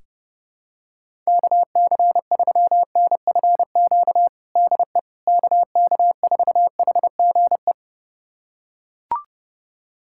Morse code, heard as a keyed tone, about 30 wpm, 700 Hz. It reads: KC3NFQ DE KK4HGE
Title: USA Callsign DE Callsign - 6 Characters - 30wpm